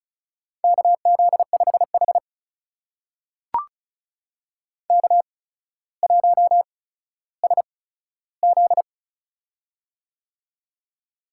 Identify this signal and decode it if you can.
Morse code, heard as a keyed tone, about 36 wpm, spaced as for 15 wpm, 700 Hz. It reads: KZ5H K1SZ
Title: USA Call Signs - 4 Characters - 8x Character Spacing - 35wpm